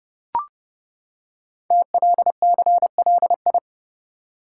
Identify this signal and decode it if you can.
Morse code, heard as a keyed tone, about 30 wpm, 700 Hz. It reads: TLCLI
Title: Code Groups - 5 Letter:Numbers - 30wpm